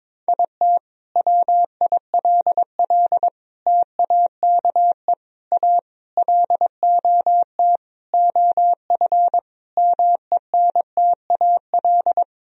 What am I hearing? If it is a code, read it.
Morse code, heard as a keyed tone, about 22 wpm, 700 Hz. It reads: IT WILL TAKE A LOT OF MENTAL
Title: Sentences from Top 1000 Words - Review Part 4 - 22wpm